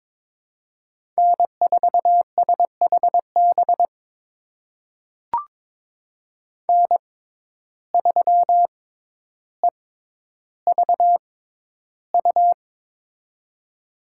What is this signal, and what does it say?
Morse code, heard as a keyed tone, about 22 wpm, spaced as for 11 wpm, 700 Hz. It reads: N4SHB N3EVU
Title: USA Call Signs - 5 Characters - 6x Character Spacing - 22wpm